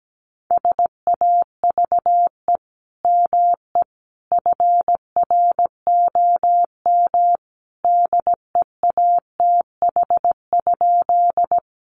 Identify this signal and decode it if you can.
Morse code, heard as a keyed tone, about 17 wpm, 700 Hz. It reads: SAVE ME FROM DEATH?
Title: Sentences from Top 1100 Words 17wpm